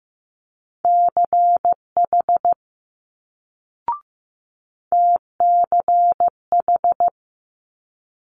Morse code, heard as a keyed tone, about 15 wpm, 700 Hz. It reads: CH TCH
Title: Multi-letter Phonemes Quick Fire 15wpm